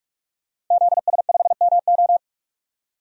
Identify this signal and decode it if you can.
Morse code, heard as a keyed tone, about 46 wpm, 700 Hz. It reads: ZS5MO
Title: DXCC-100 Call Signs - 5 Characters - 45wpm